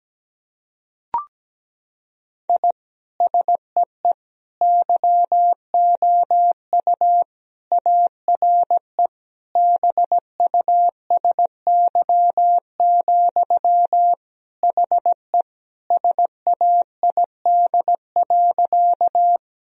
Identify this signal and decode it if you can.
Morse code, heard as a keyed tone, about 17 wpm, 700 Hz. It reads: I SEE YOU ARE BUSY, HE SAID.
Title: Sentences from Top 1000 Words - Review Part 3 - 17wpm